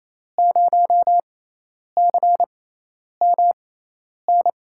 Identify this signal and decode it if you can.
Morse code, heard as a keyed tone, about 28 wpm, spaced as for 10 wpm, 700 Hz. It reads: 0CMN
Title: USA Call Signs - 6 Characters - 6x Character Spacing - 28wpm